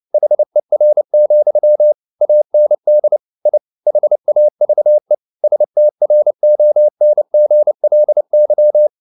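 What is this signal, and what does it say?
Morse code, heard as a keyed tone, about 29 wpm, 600 Hz. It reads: HER, AND I HAVE STRONGLY
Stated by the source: Morse code at 29 wpm, sending Ken Follett's novel Never in English, from CW med Gnister.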